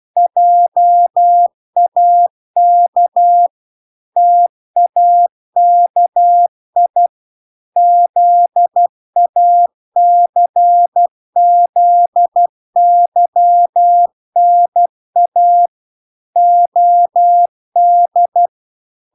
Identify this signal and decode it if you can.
Morse code, heard as a keyed tone, about 12 wpm, 700 Hz. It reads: JAK TAKI ZACZYNA OD